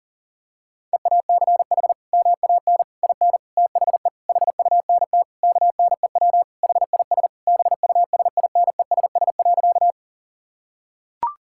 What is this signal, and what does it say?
Morse code, heard as a keyed tone, about 40 wpm, 700 Hz. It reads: EACH MAN IN THE HUNT KNEW HIS BUSINESS.
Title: Sentences from Top 1600 Words 40wpm